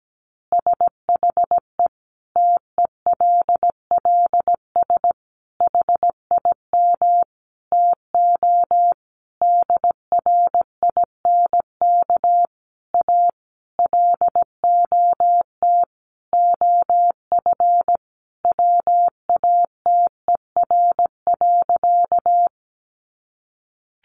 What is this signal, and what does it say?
Morse code, heard as a keyed tone, about 17 wpm, 700 Hz. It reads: SHE TELLS HIM TO DRINK A LOT OF WATER.
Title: Additional Easy Phrases 17wpm